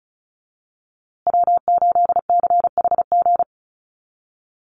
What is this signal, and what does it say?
Morse code, heard as a keyed tone, about 34 wpm, 700 Hz. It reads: W8CHG
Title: USA Call Signs Repeated 3x 35wpm